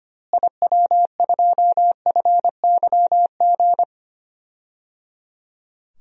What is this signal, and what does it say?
Morse code, heard as a keyed tone, about 25 wpm, 700 Hz. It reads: IW2FYG
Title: DXCC-100 Call Signs - 6 Characters - Repeated 3x - 25wpm